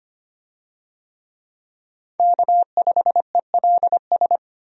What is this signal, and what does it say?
Morse code, heard as a keyed tone, about 25 wpm, 700 Hz. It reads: K5ELS
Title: USA Call Signs - 5 Characters - 25wpm